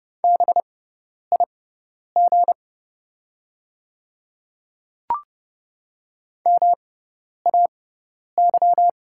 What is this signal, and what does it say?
Morse code, heard as a keyed tone, about 30 wpm, spaced as for 10 wpm, 700 Hz. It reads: BIG MAY
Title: 6x Character Spacing - Top 200 Words - 30wpm